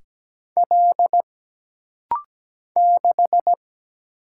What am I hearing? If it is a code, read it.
Morse code, heard as a keyed tone, about 17 wpm, 700 Hz. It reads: L 6
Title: Single Letter-Number Alt - Rapid-Fire 17wpm